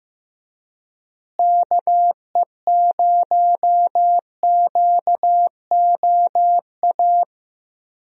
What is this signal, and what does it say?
Morse code, heard as a keyed tone, about 15 wpm, 700 Hz. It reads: KE0QOA
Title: USA Call Signs - 6 Characters - Repeated 3x - 15wpm